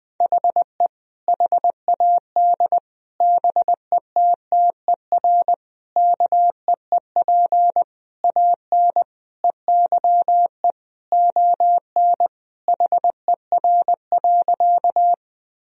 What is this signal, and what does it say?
Morse code, heard as a keyed tone, about 20 wpm, 700 Hz. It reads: HE HAD BETTER KEEP AN EYE ON HER.
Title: Sentences from Top 1000 Words - Review Part 1 - 20wpm